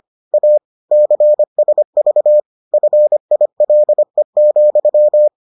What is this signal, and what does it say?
Morse code, heard as a keyed tone, about 25 wpm, 600 Hz. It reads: A CSV FILE,